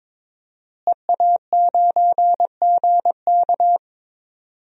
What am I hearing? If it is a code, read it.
Morse code, heard as a keyed tone, about 22 wpm, 700 Hz. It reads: EA9GK
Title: DXCC-100 Call Signs - 5 Characters - Repeated 2x - 22wpm